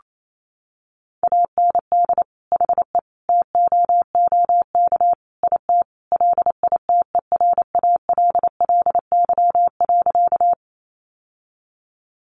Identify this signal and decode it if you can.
Morse code, heard as a keyed tone, about 28 wpm, 700 Hz. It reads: AND HE TOOK IT LITERALLY.
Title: Sentences from Top 1400 Words 28wpm